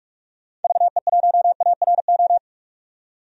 Morse code, heard as a keyed tone, about 46 wpm, 700 Hz. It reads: VE1ARO
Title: DXCC-100 Call Signs - 6 Characters - 45wpm